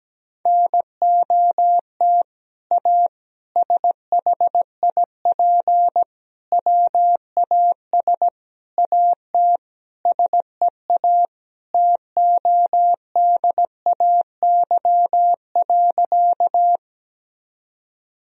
Morse code, heard as a keyed tone, about 17 wpm, 700 Hz. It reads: NOT A SHIP WAS AT SEA TODAY.
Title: Sentences from Top 1000 Words - Review Part 2 - 17wpm